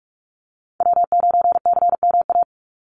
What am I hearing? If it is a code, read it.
Morse code, heard as a keyed tone, about 44 wpm, 700 Hz. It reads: W9CMA
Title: USA Call Signs 45wpm